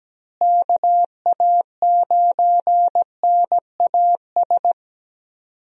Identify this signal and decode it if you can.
Morse code, heard as a keyed tone, about 17 wpm, 700 Hz. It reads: KA9NAS
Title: USA Call Signs - 6 Characters - Repeated 2x - 17wpm